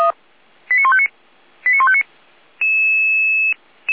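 Five-tone selective call (ZVEI), heard as quick runs of five tones, tones of about 70 ms. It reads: 89159 89159
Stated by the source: Five-tone ZVEI selective call received on an SDR receiver.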